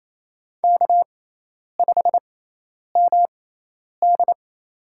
Morse code, heard as a keyed tone, about 28 wpm, spaced as for 10 wpm, 700 Hz. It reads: K5MD
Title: USA Call Signs - 6x Character Spacing - 28wpm